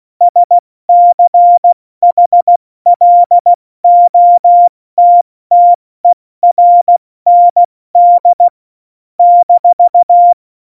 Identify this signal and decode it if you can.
Morse code, heard as a keyed tone, about 16 wpm, 700 Hz. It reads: SCHLOTTERND -
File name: Morsekurs / Witze / Witz 07. Januar 2020, 70 F80